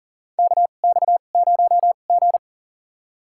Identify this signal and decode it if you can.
Morse code, heard as a keyed tone, about 40 wpm, spaced as for 29 wpm, 700 Hz. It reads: KX0G
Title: USA Call Signs - 2x Character Spacing - 40wpm